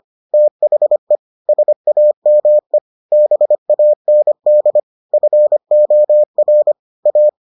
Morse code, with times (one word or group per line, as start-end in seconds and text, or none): 0.3-1.2 THE
1.5-2.8 SAME
3.1-4.8 BAND
5.1-6.7 FOR
7.1-7.3 A